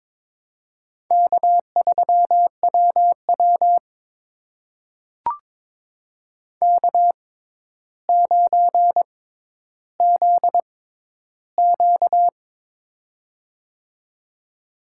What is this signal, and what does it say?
Morse code, heard as a keyed tone, about 22 wpm, spaced as for 11 wpm, 700 Hz. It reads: K3WW K9ZQ
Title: USA Call Signs - 4 Characters - 6x Character Spacing - 22wpm